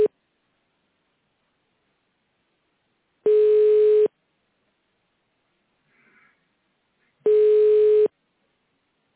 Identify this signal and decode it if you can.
Ringback tone (425 Hz, 1 s on, 3 s off).